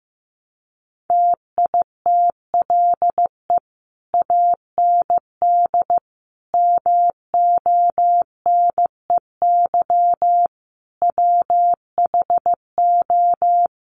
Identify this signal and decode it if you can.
Morse code, heard as a keyed tone, about 15 wpm, 700 Hz. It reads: TITLE AND MONEY WHO
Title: Sentences from Top 1200 Words 15wpm